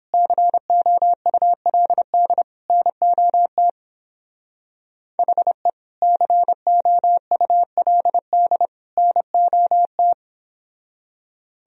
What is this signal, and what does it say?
Morse code, heard as a keyed tone, about 29 wpm, 700 Hz. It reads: COULD NOT HE COULD NOT